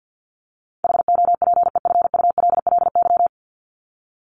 Morse code, heard as a keyed tone, about 50 wpm, 700 Hz. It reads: HOPEFULLY
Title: Top 1251-1500 Words 50wpm